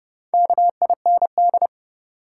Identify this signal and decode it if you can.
Morse code, heard as a keyed tone, about 30 wpm, 700 Hz. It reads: KIND
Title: Sets of 2 Words - New Word Drill - 30wpm